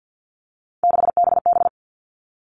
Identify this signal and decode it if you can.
Morse code, heard as a keyed tone, about 48 wpm, 700 Hz. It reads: 6BB
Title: All Cut Numbers - 3-Digit 50wpm